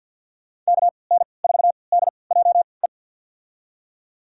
Morse code, heard as a keyed tone, about 51 wpm, spaced as for 28 wpm, 690 Hz. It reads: KN4DJE
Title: USA Call Signs - 6 Characters - 3x Character Spacing - 50wpm